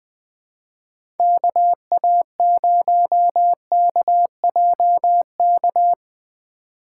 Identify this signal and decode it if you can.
Morse code, heard as a keyed tone, about 20 wpm, 700 Hz. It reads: KA0KJK
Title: USA Call Signs - 6 Characters - 4x Character Spacing - 20wpm